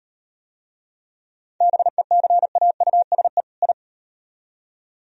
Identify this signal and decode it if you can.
Morse code, heard as a keyed tone, about 38 wpm, 700 Hz. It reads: BECAUSE I